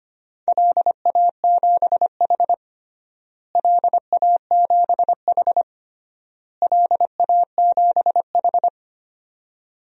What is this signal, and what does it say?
Morse code, heard as a keyed tone, about 25 wpm, 700 Hz. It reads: LA7H LA7H LA7H
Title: DXCC-100 Call Signs - 4 Characters - Repeated 3x - 25wpm